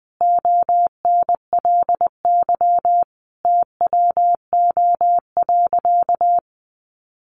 Morse code, heard as a keyed tone, about 19 wpm, 700 Hz. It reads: ONLY TWO.
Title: Sentences from Top 200 Words 20wpm